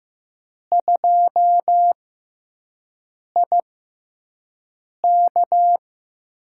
Morse code, heard as a keyed tone, about 15 wpm, spaced as for 5 wpm, 700 Hz. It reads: 2IK
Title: USA Call Signs - 6x Character Spacing - 15wpm